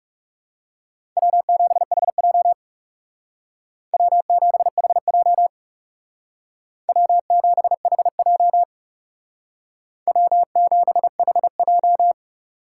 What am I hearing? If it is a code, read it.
Morse code, heard as a keyed tone, about 38 wpm, 700 Hz. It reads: W7HJ W7HJ W7HJ W7HJ